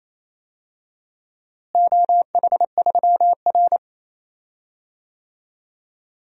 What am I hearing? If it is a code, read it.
Morse code, heard as a keyed tone, about 28 wpm, 700 Hz. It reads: OH3R
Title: DXCC-100 Call Signs - 4 Characters - Repeated 2x - 28wpm